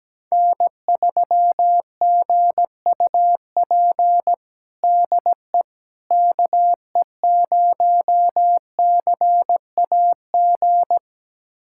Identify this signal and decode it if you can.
Morse code, heard as a keyed tone, about 17 wpm, 700 Hz. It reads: N3GUP DE KE0CAG
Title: USA Callsign DE Callsign 17wpm